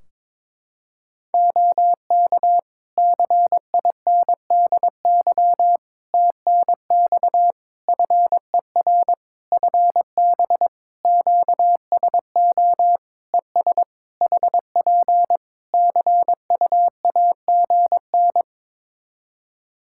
Morse code, heard as a keyed tone, about 22 wpm, 700 Hz. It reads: OK CINDY TNX FER FB QSO ES HP CUAGN
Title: K1USN SST QSO Practice 22wpm